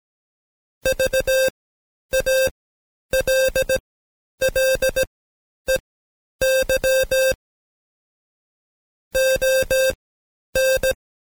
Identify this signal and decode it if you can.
Morse code, heard as a keyed tone, about 17 wpm, spaced as for 9 wpm, 530 Hz. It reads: VALLEY ON
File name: Code Quick / AudioDownld2 / Seg131-Stepper4 9WPM